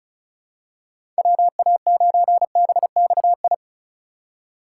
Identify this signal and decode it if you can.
Morse code, heard as a keyed tone, about 36 wpm, 700 Hz. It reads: WA9BXI
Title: USA Call Signs - 6 Characters - 4x Character Spacing - 35wpm